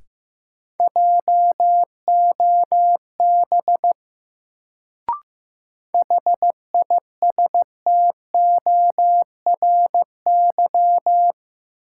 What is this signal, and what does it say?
Morse code, heard as a keyed tone, about 15 wpm, 700 Hz. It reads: JOB HISTORY
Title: Sentences from Top 600 Words - New Word Drill - 15wpm